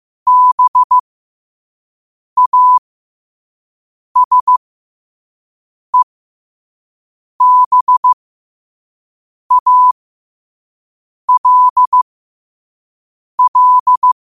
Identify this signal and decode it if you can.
Morse code, heard as a keyed tone, about 15 wpm, spaced as for 5 wpm, 1000 Hz. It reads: BASEBALL
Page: Quotes for Thu, 14 Aug 2025 in Morse Code at 5 words per minute.